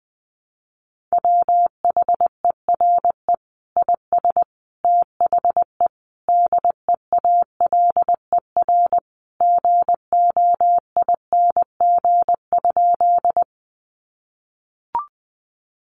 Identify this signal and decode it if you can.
Morse code, heard as a keyed tone, about 20 wpm, 700 Hz. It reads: WHERE IS THE DEALER GOING?
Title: Sentences from Top 1100 Words 20wpm